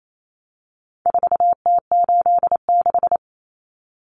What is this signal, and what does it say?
Morse code, heard as a keyed tone, about 28 wpm, 700 Hz. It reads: 4T86
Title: Common Cut Numbers - 4-Digit 28wpm